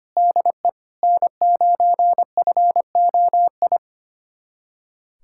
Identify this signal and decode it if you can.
Morse code, heard as a keyed tone, about 25 wpm, 700 Hz. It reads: DE N9FOI
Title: USA Callsign DE Callsign - 5 Characters - 25wpm